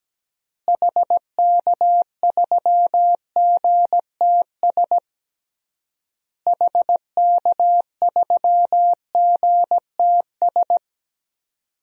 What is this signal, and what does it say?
Morse code, heard as a keyed tone, about 17 wpm, 700 Hz. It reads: HK3GTS HK3GTS
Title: DXCC-100 Call Signs - 6 Characters - Repeated 3x - 17wpm